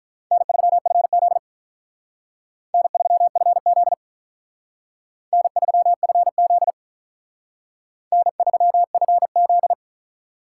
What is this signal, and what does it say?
Morse code, heard as a keyed tone, about 45 wpm, 700 Hz. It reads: N3FZ N3FZ N3FZ N3FZ